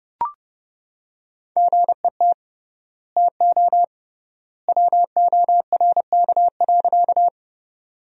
Morse code, heard as a keyed tone, about 30 wpm, 700 Hz. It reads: GET TO WORK.
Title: 3x Word-Spacing - Sentences from Top 200 Words - 30wpm